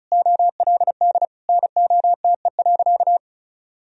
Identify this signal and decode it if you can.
Morse code, heard as a keyed tone, about 34 wpm, 700 Hz. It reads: OLD NOTE.
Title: Sentences from Top 1000 Words - Review Part 2 - 35wpm